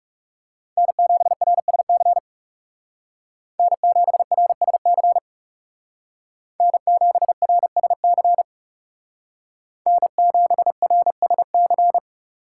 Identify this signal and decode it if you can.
Morse code, heard as a keyed tone, about 36 wpm, 700 Hz. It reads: N7RSC N7RSC N7RSC N7RSC
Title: Speed Racing - USA Call Signs 30wpm